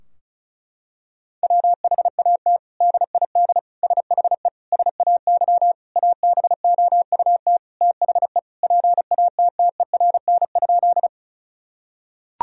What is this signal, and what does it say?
Morse code, heard as a keyed tone, about 34 wpm, 700 Hz. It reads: WHAT DID SHE SAY ABOUT THE PATTERN?
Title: Sentences from Top 1000 Words - Review Part 2 - 35wpm